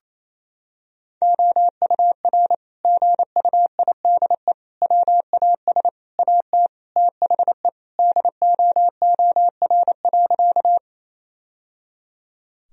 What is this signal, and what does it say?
Morse code, heard as a keyed tone, about 28 wpm, 700 Hz. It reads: OUR GUIDE WAS AT THE DOOR.
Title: Sentences from Top 1000 Words - Review Part 3 - 28wpm